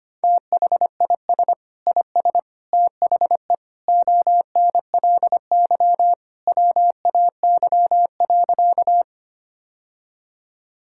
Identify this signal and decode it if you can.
Morse code, heard as a keyed tone, about 25 wpm, 700 Hz. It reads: THIS IS THE ONLY WAY.
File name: practice / 25wpm / Sentences with 2-Word Phrases - 25wpm